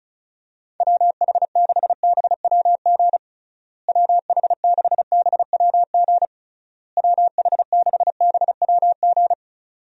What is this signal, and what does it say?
Morse code, heard as a keyed tone, about 34 wpm, 700 Hz. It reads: WH6BWG WH6BWG WH6BWG
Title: USA Call Signs - 6 Characters - Repeated 3x - 35wpm